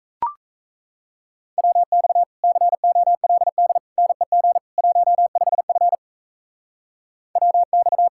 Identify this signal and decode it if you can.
Morse code, heard as a keyed tone, about 35 wpm, 700 Hz. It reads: WX COLD NEG 15F WX
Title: Speed-Racing Common QSO Phrases - 28wpm